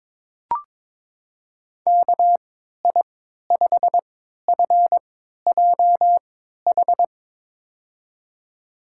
Morse code, heard as a keyed tone, about 22 wpm, spaced as for 12 wpm, 700 Hz. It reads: KI5FJH
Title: USA Call Signs - 6 Characters - 3x Character Spacing - 22wpm